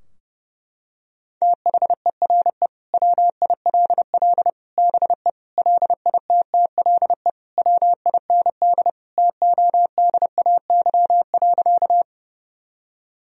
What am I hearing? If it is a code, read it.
Morse code, heard as a keyed tone, about 30 wpm, 700 Hz. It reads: THERE WILL BE LITTLE WIND TODAY.
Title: Sentences from Top 1000 Words - Review Part 2 - 30wpm